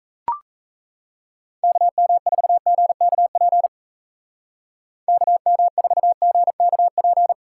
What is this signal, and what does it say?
Morse code, heard as a keyed tone, about 34 wpm, 700 Hz. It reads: KM4GKP KM4GKP
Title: Speed Racing - USA Call Signs 28wpm